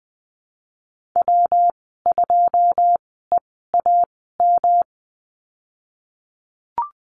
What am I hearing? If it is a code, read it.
Morse code, heard as a keyed tone, about 20 wpm, 700 Hz. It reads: W 2 E A M
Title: 154 Lesson 25g - Letter p - Call Signs 20wpm